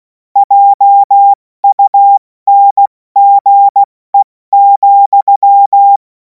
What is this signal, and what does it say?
Morse code, heard as a keyed tone, about 16 wpm, 800 Hz. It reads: JUNGE,
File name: Morsekurs / Witze / Witz 24. März 2026 70 F80